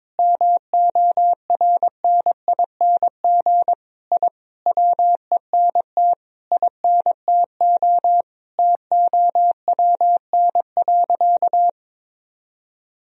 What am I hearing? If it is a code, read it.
Morse code, heard as a keyed tone, about 22 wpm, 700 Hz. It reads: MORNING I WENT INTO TOWN.
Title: Sentences from Top 1000 Words - Review Part 2 - 22wpm